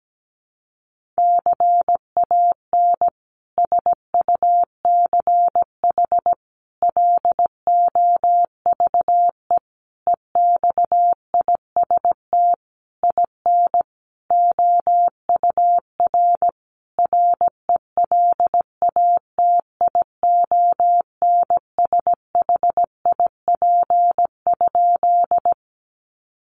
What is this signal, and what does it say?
Morse code, heard as a keyed tone, about 17 wpm, 700 Hz. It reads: CAN SUCH LOVE EXIST IN OUR RELATIONSHIP?
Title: Sentences from Top 1100 Words 17wpm